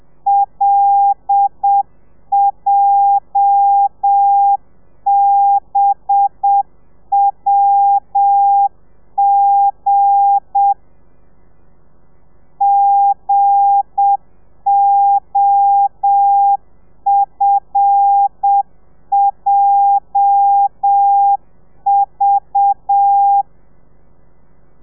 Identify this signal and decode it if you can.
Morse code, heard as a keyed tone, about 7 wpm, 800 Hz. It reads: LJBWG GOFJV
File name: telegrafia / i2tfj / Lez8 35